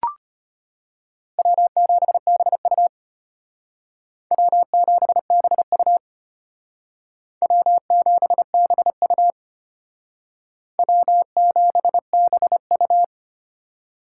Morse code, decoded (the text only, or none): W7BU W7BU W7BU W7BU